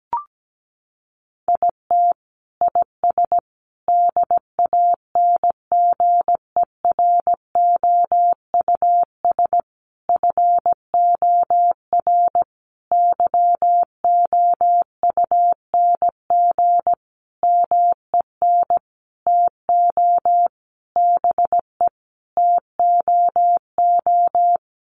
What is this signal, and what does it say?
Morse code, heard as a keyed tone, about 17 wpm, 700 Hz. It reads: IT IS DANGEROUS FOR YOUNG MEN TO BE TOO
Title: Sentences from Top 1200 Words 17wpm